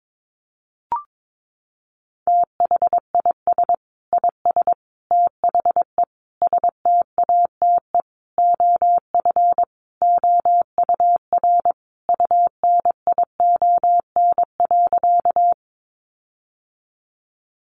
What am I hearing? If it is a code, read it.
Morse code, heard as a keyed tone, about 22 wpm, 700 Hz. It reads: THIS IS THE STATE OF OUR UNION.
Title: Sentences from Top 1400 Words 22wpm